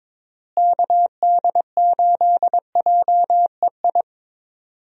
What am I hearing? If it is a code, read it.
Morse code, heard as a keyed tone, about 22 wpm, 700 Hz. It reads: KD8JEI
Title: USA Call Signs - 6 Characters - Repeated 3x - 22wpm